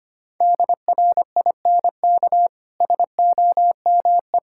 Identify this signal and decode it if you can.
Morse code, heard as a keyed tone, about 25 wpm, 700 Hz. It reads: DRINK SOME
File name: Easy Sentences with Hints - 25wpm